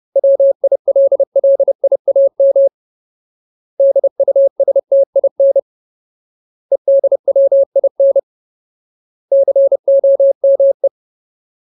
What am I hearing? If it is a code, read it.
Morse code, heard as a keyed tone, about 30 wpm, 550 Hz. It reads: WILLIAM DUSTIN EDWIN COME
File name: Ultimate Word List / 30WPM Ultimate